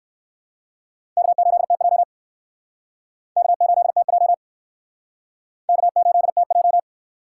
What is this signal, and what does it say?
Morse code, heard as a keyed tone, about 60 wpm, 700 Hz. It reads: K8TJ K8TJ K8TJ